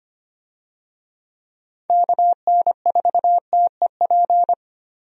Morse code, heard as a keyed tone, about 25 wpm, 700 Hz. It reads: KN4TEP
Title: USA Call Signs - 6 Characters - Repeated 3x - 25wpm